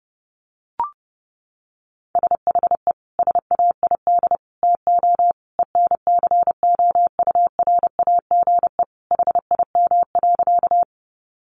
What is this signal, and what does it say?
Morse code, heard as a keyed tone, about 30 wpm, 700 Hz. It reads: SHE SAID TO ENCOURAGE HIM.
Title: Sentences from Top 1400 Words 30wpm